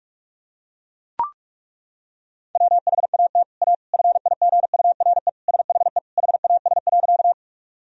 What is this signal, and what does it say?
Morse code, heard as a keyed tone, about 42 wpm, 700 Hz. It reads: WHAT A FIGURE SHE HAS.
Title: Sentences from Top 1000 Words - Review Part 2 - 45wpm